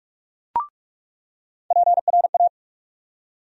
Morse code, heard as a keyed tone, about 46 wpm, 700 Hz. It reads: PRA
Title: USA Call Sign Postfixes - 3 Letters 45wpm